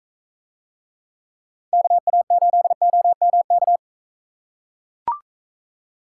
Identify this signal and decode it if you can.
Morse code, heard as a keyed tone, about 41 wpm, 700 Hz. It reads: KA8OMK